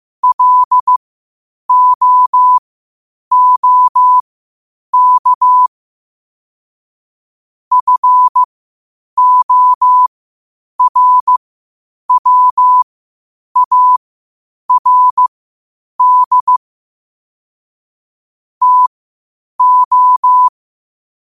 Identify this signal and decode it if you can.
Morse code, heard as a keyed tone, about 15 wpm, spaced as for 8 wpm, 1000 Hz. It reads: LOOK FORWARD TO
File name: qotdfiles / 8wpm